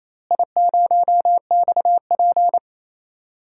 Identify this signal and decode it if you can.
Morse code, heard as a keyed tone, about 28 wpm, 700 Hz. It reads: I0XP
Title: DXCC-100 Call Signs - 4 Characters - Repeated 2x - 28wpm